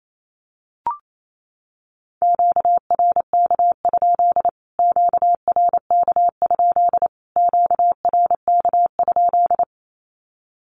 Morse code, heard as a keyed tone, about 28 wpm, 700 Hz. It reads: QRK? QRK? QRK?
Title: Q-Codes Repeated 3x 28wpm